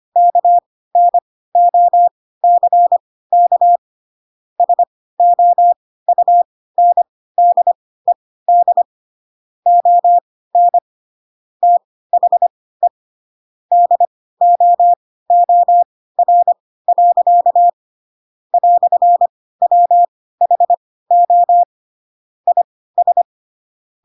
Morse code, heard as a keyed tone, about 25 wpm, spaced as for 16 wpm, 700 Hz. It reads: KNOCK SOUNDED ON THE DOOR. "WHO IS